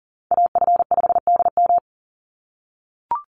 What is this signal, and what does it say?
Morse code, heard as a keyed tone, about 40 wpm, 700 Hz. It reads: AF5DM
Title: Call Signs - 40wpm